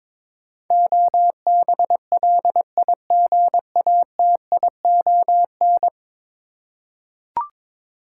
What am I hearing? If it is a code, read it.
Morse code, heard as a keyed tone, about 22 wpm, 700 Hz. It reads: OBLIGATION
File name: practice / 22wpm / 10 Letter Words - First 3 Letters Given - 22wpm